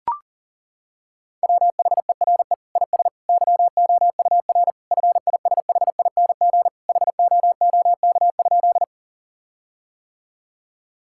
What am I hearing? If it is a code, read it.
Morse code, heard as a keyed tone, about 40 wpm, 700 Hz. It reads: WHERE IS YOUR FISHING HOOK?
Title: Sentences from Top 1600 Words 40wpm